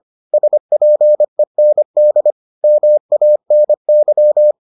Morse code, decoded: SPEND MANY

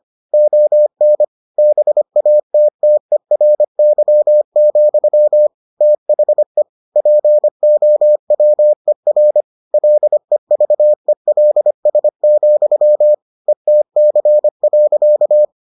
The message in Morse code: ON BATTERY, THE POWER LEVELS, ETC.

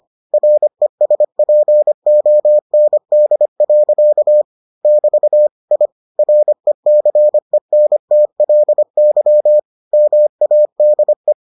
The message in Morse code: RESPOND. <BT> I RECENTLY MADE

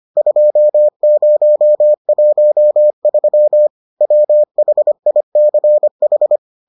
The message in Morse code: 2013 WHICH

